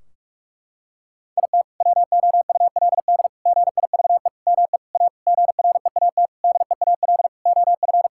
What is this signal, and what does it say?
Morse code, heard as a keyed tone, about 46 wpm, 700 Hz. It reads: IT WOULD GIVE ME A GREAT DEAL OF
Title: Sentences from Top 1600 Words 45wpm